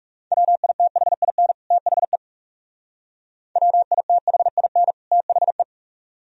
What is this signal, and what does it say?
Morse code, heard as a keyed tone, about 36 wpm, 700 Hz. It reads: WITHIN THE WITHIN THE
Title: Speed-Racing - Sets of 2 Words 30wpm